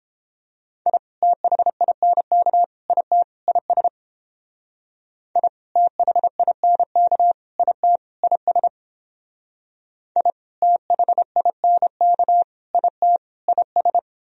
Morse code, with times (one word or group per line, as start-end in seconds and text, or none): 0.9-1.0 I
1.2-2.6 THINK
2.9-3.2 IT
3.5-3.9 IS
5.4-5.5 I
5.8-7.3 THINK
7.6-8.0 IT
8.2-8.7 IS
10.2-10.3 I
10.6-12.4 THINK
12.7-13.2 IT
13.5-14.0 IS